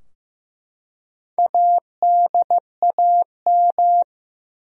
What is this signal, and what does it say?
Morse code, heard as a keyed tone, about 15 wpm, 700 Hz. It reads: ADAM
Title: USA Common Names - Max 4 Letters - 15wpm